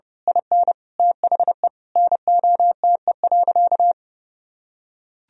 Morse code, heard as a keyed tone, about 30 wpm, 700 Hz. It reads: IN THE NOTE.
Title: Sentences from Top 1000 Words - Review Part 2 - 30wpm